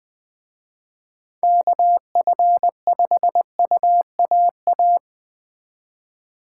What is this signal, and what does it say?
Morse code, heard as a keyed tone, about 21 wpm, 700 Hz. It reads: KF5UAA
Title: Speed Racing - USA 2x3 Call Signs 17wpm